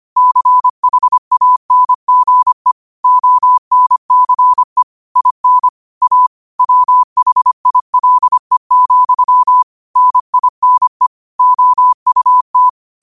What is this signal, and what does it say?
Morse code, heard as a keyed tone, about 25 wpm, 1000 Hz. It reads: CHANGE ONCE IN A WHILE, NINE OUT
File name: qotdfiles / 25wpm